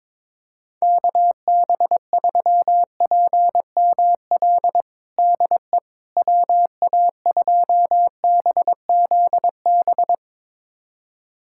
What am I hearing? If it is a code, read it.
Morse code, heard as a keyed tone, about 22 wpm, 700 Hz. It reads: KB3PML DE WA2BZB
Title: USA Callsign DE Callsign - 6 Characters - 22wpm